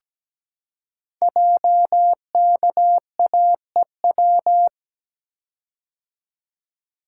Morse code, heard as a keyed tone, about 17 wpm, 700 Hz. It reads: JKAEW
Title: Code Groups - 5 Letters - 17wpm